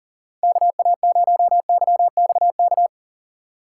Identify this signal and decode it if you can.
Morse code, heard as a keyed tone, about 40 wpm, 700 Hz. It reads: KA0YXK
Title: USA Call Signs - 6 Characters - 3x Character Spacing - 40wpm